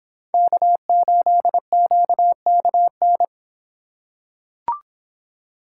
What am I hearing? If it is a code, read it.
Morse code, heard as a keyed tone, about 26 wpm, 700 Hz. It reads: K8QKN